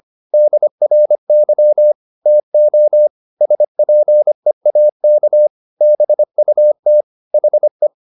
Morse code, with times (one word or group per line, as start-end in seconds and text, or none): 0.3-1.9 DRY
2.3-3.1 TO
3.4-5.5 SPEAK
5.8-7.0 BUT
7.3-7.9 HE